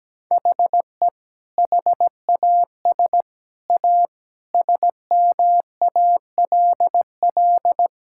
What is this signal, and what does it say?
Morse code, heard as a keyed tone, about 17 wpm, 700 Hz. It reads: HE HAS A SMALL
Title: Sentences from Top 1000 Words - Review Part 6 - 17wpm